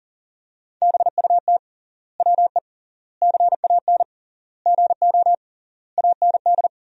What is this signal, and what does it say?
Morse code, heard as a keyed tone, about 40 wpm, 700 Hz. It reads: BUT WE CAN GO AND
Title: 3x Word-Spacing - Sentences from Top 100 Words - 40wpm